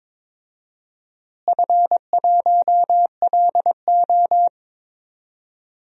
Morse code, heard as a keyed tone, about 22 wpm, 700 Hz. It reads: F1LO
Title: DXCC-100 Call Signs - 4 Characters - Repeated 2x - 22wpm